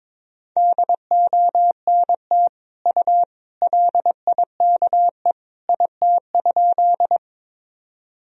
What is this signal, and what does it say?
Morse code, heard as a keyed tone, about 22 wpm, 700 Hz. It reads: DONT U LIKE IT?
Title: Sentences from Top 300 Words and Abbreviations 22wpm